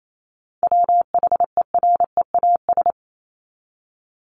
Morse code, heard as a keyed tone, about 28 wpm, 700 Hz. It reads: WHEREAS
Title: Top 1501-1750 Words 28wpm